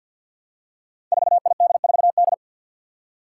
Morse code, heard as a keyed tone, about 50 wpm, 700 Hz. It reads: 4ID4D
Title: Five Challenging Letter-Number Combinations 50wpm